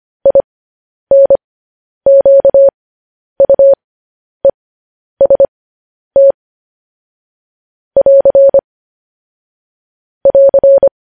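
Morse code, heard as a keyed tone, about 25 wpm, spaced as for 10 wpm, 550 Hz. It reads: INQUEST <AR> <AR>
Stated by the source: Morse code news headlines practice at 10 wpm.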